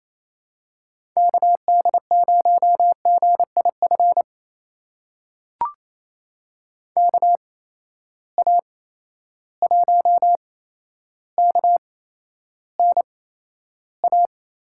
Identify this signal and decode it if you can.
Morse code, heard as a keyed tone, about 28 wpm, spaced as for 12 wpm, 700 Hz. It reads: KD0GIF KA1KNA
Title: USA Call Signs - 6 Characters - 8x Character Spacing - 28wpm